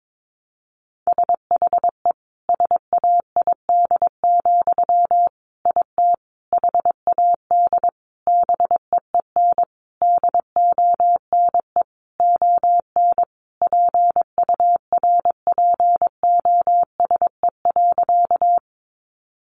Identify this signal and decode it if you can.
Morse code, heard as a keyed tone, about 22 wpm, 700 Hz. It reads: SHE SAID, IT HAD BEEN DONE ON PURPOSE.
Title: Sentences from Top 800 Words 22wpm